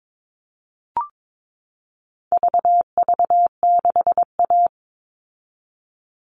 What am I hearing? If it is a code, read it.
Morse code, heard as a keyed tone, about 22 wpm, 700 Hz. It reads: VV6A
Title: All Cut Numbers - 4-Digit 22wpm